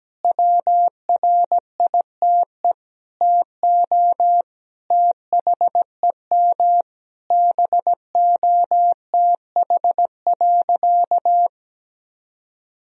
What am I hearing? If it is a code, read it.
Morse code, heard as a keyed tone, about 17 wpm, 700 Hz. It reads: WRITE TO THEM BOTH.
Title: Sentences from Top 1000 Words - Review Part 1 - 17wpm